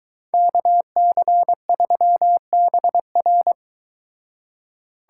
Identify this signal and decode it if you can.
Morse code, heard as a keyed tone, about 22 wpm, 700 Hz. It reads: KC3BR